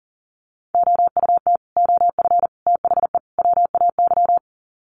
Morse code, heard as a keyed tone, about 40 wpm, 700 Hz. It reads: OUT OF THE WAY
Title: Sets of 4 Words 40wpm